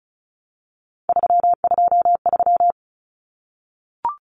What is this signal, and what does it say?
Morse code, heard as a keyed tone, about 34 wpm, 700 Hz. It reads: 323
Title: Common Cut Numbers - 3-Digit 35wpm